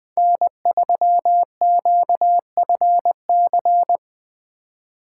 Morse code, heard as a keyed tone, about 22 wpm, 700 Hz. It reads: N3QFC
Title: Speed Racing - USA 1x3 Call Signs 20wpm